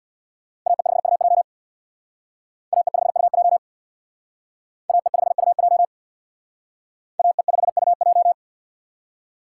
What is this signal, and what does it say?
Morse code, heard as a keyed tone, about 62 wpm, 700 Hz. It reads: AE5UJ AE5UJ AE5UJ AE5UJ
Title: Speed Racing - USA 2x2 Call Signs 50wpm